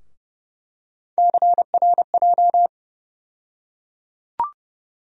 Morse code, heard as a keyed tone, about 30 wpm, 700 Hz. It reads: CRJ
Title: USA Call Sign Postfixes - 3 Letters 30wpm